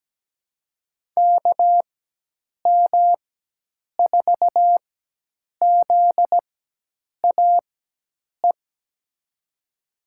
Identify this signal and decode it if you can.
Morse code, heard as a keyed tone, about 17 wpm, spaced as for 8 wpm, 700 Hz. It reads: KM4ZAE
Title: USA Call Signs - 6 Characters - 4x Character Spacing - 17wpm